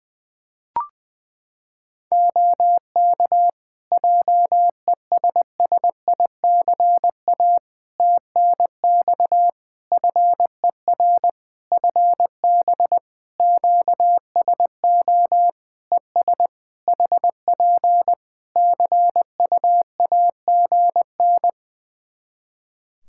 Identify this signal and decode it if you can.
Morse code, heard as a keyed tone, about 21 wpm, 700 Hz. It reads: OK JESSICA TNX FER FB QSO ES HP CUAGN
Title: Practice QSO - Exchange 4 - Ending - 20wpm